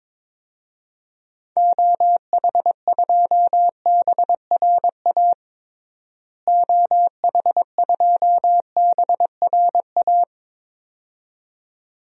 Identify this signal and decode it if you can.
Morse code, heard as a keyed tone, about 22 wpm, 700 Hz. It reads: OH2BRA OH2BRA
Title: DXCC-100 Call Signs - 6 Characters - Repeated 2x - 22wpm